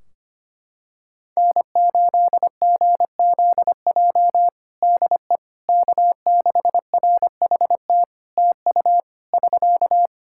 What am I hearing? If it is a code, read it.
Morse code, heard as a keyed tone, about 25 wpm, 700 Hz. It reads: N8GZJ DE K6RHT TU <SK>
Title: Practice QSO - Exchange 4 - Ending - 25wpm